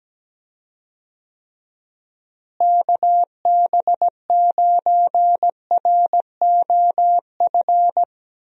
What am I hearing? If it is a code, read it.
Morse code, heard as a keyed tone, about 17 wpm, 700 Hz. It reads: KB9ROF
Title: USA Call Signs - 6 Characters - Repeated 3x - 17wpm